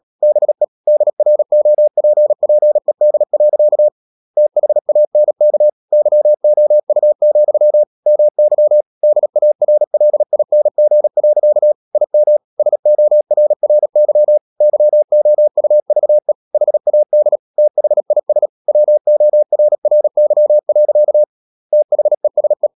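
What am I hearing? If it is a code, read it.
Morse code, heard as a keyed tone, about 37 wpm, 600 Hz. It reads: BE DROPPED. THANK YOU, MY DARLING. IM SORRY YOUVE HAD THIS WORRY. THESE
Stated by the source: Morse code practice at 35–39 wpm sending English text of Ken Follett's novel Never, from CW med Gnister.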